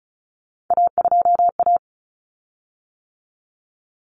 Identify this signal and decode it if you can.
Morse code, heard as a keyed tone, about 34 wpm, 700 Hz. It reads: A2A
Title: Common Cut Numbers - 3-Digit 35wpm